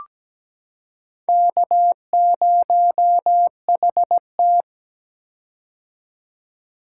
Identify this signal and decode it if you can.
Morse code, heard as a keyed tone, about 17 wpm, 700 Hz. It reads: K0HT
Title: K1USN SST QSO Practice - 17wpm